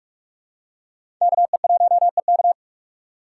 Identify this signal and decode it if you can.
Morse code, heard as a keyed tone, about 41 wpm, 700 Hz. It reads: KE1EK
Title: USA Call Signs - 5 Characters - 3x Character Spacing - 45wpm